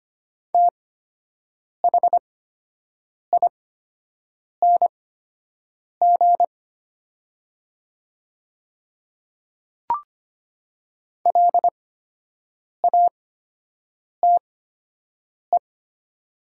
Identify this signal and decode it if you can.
Morse code, heard as a keyed tone, about 25 wpm, spaced as for 7 wpm, 700 Hz. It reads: THING LATE
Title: 8x Character Spacing - Top 300 Words - 25wpm